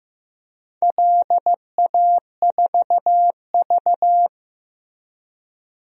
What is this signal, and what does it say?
Morse code, heard as a keyed tone, about 15 wpm, 700 Hz. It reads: LA4V
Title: DXCC-100 Call Signs - 4 Characters - Repeated 3x - 15wpm